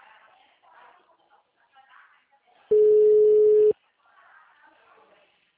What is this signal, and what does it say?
Ringback tone (Europe, 425 Hz), with speech.